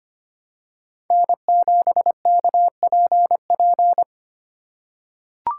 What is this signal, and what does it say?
Morse code, heard as a keyed tone, about 25 wpm, 700 Hz. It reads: N7KPP
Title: USA Call Signs - 5 Characters - 3x Character Spacing - 25wpm